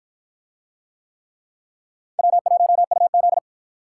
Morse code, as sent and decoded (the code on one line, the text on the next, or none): .-- .---- ..- --..
W1UZ